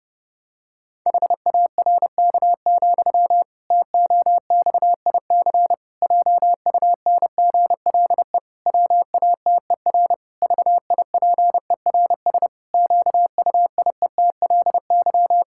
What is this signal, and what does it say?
Morse code, heard as a keyed tone, about 30 wpm, 700 Hz. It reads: HARK, TOXIC JUNGLE WATER VIPERS QUIETLY
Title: Pangram Challenge v2 - 30wpm